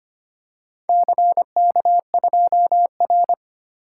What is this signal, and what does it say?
Morse code, heard as a keyed tone, about 25 wpm, 700 Hz. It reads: CK2R
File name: practice / 25wpm / Four Letter-Number Combinations - 25wpm